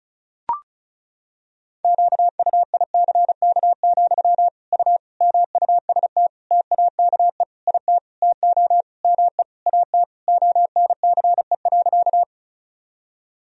Morse code, heard as a keyed tone, about 36 wpm, 700 Hz. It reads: QUICK, U MUST TAKE IT TO ME AT ONCE.
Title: Sentences from Top 500 Words and Abbreviations 35wpm